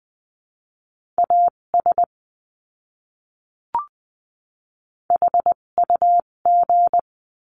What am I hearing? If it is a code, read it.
Morse code, heard as a keyed tone, about 20 wpm, spaced as for 17 wpm, 700 Hz. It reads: AS HUG
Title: 257 Lesson 42b - Words 14-20wpm